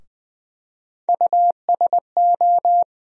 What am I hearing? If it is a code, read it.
Morse code, heard as a keyed tone, about 20 wpm, 700 Hz. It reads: USO
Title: USA Call Sign Postfixes - 3 Letters 20wpm